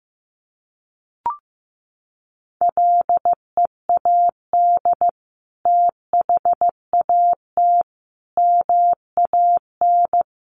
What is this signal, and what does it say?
Morse code, heard as a keyed tone, about 15 wpm, 700 Hz. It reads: LEAD THAT MAN
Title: Sentences from Top 600 Words 15wpm